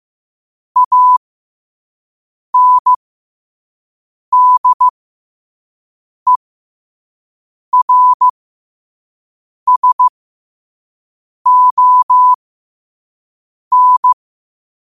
Morse code, heard as a keyed tone, about 15 wpm, spaced as for 5 wpm, 1000 Hz. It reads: ANDERSON